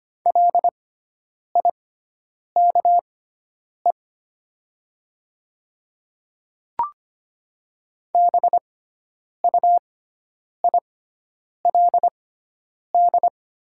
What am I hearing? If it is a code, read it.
Morse code, heard as a keyed tone, about 25 wpm, spaced as for 9 wpm, 700 Hz. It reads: LIKE BUILD
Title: 6x Character Spacing - Top 300 Words - 25wpm